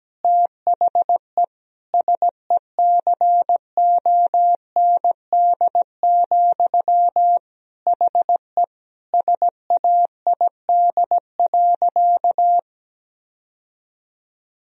Morse code, heard as a keyed tone, about 17 wpm, 700 Hz. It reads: THE SECOND, HE SAID.
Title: Sentences from Top 1000 Words - Review Part 1 - 17wpm